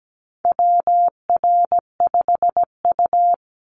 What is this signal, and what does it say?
Morse code, heard as a keyed tone, about 17 wpm, 700 Hz. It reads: WR5U
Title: Call Signs - 17wpm